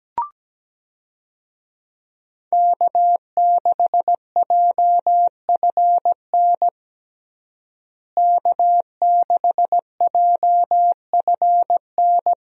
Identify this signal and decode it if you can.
Morse code, heard as a keyed tone, about 17 wpm, 700 Hz. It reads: K6JFN K6JFN
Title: USA Call Signs - 5 Characters - Repeated 2x - 17wpm